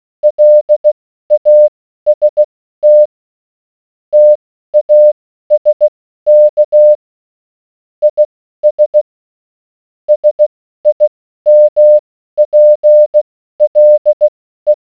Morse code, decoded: LAST TASK IS SIMPLE